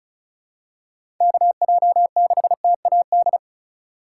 Morse code, heard as a keyed tone, about 36 wpm, 700 Hz. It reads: KJ6TAD
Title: USA Call Signs - 6 Characters - 4x Character Spacing - 35wpm